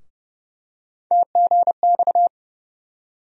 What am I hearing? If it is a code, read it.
Morse code, heard as a keyed tone, about 30 wpm, 700 Hz. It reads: TGX
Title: USA Call Sign Postfixes - 3 Letters 30wpm